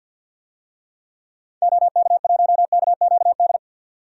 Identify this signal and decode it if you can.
Morse code, heard as a keyed tone, about 51 wpm, 700 Hz. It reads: OK1KQD
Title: DXCC-100 Call Signs - 6 Characters - 50wpm